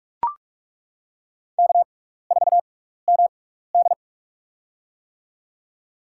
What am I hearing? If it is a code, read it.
Morse code, heard as a keyed tone, about 42 wpm, spaced as for 15 wpm, 700 Hz. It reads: K4MD
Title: USA Call Signs - 4 Characters - 6x Character Spacing - 45wpm